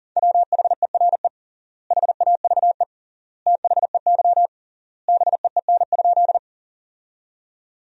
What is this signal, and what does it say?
Morse code, heard as a keyed tone, about 40 wpm, 700 Hz. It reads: WHERE HAVE THEY BEEN?
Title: 3x Word-Spacing - Sentences from Top 200 Words - 40wpm